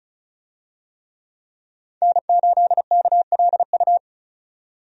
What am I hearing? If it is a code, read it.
Morse code, heard as a keyed tone, about 36 wpm, 700 Hz. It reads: N8KLU
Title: USA Call Signs - 5 Characters - 35wpm